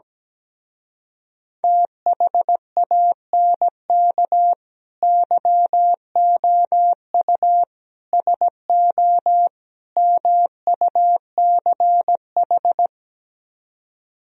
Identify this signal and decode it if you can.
Morse code, heard as a keyed tone, about 17 wpm, 700 Hz. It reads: THANK YOU SO MUCH
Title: Speed-Racing - Sets of 4 Words 17wpm